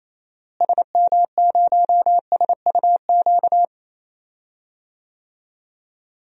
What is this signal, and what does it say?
Morse code, heard as a keyed tone, about 28 wpm, 700 Hz. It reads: SM0SUQ
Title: DXCC-100 Call Signs - 6 Characters - Repeated 3x - 28wpm